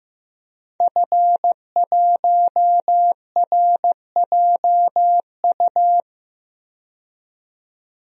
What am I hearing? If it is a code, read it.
Morse code, heard as a keyed tone, about 15 wpm, 700 Hz. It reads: F1RJU
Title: DXCC-100 Call Signs - 5 Characters - 15wpm